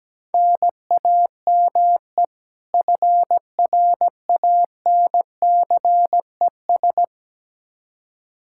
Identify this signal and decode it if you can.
Morse code, heard as a keyed tone, about 17 wpm, 700 Hz. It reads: NAME FRANCES
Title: Practice QSO - Exchange 1 - Beginning - 17wpm